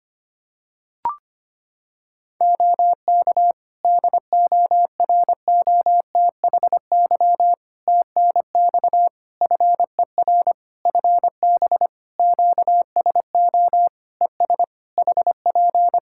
Morse code, heard as a keyed tone, about 25 wpm, 700 Hz. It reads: OK DOROTHY TNX FER FB QSO ES HP
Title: Practice QSO - Exchange 4 - Ending - 25wpm